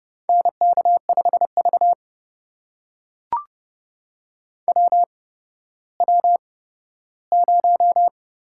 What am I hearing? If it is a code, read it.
Morse code, heard as a keyed tone, about 30 wpm, spaced as for 15 wpm, 700 Hz. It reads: NK5V WW0
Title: USA Call Signs - 8x Character Spacing - 30wpm